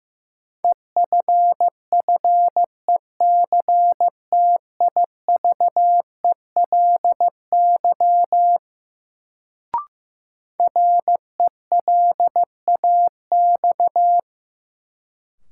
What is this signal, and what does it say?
Morse code, heard as a keyed tone, about 15 wpm, 700 Hz. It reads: EFFECTIVELY RELAX
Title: Sentences from Top 1400 Words - New Word Drill - 15wpm